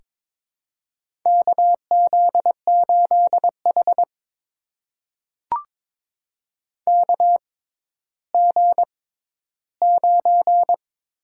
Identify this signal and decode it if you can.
Morse code, heard as a keyed tone, about 22 wpm, spaced as for 12 wpm, 700 Hz. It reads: KZ8H KG9
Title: USA Call Signs - 6x Character Spacing - 22wpm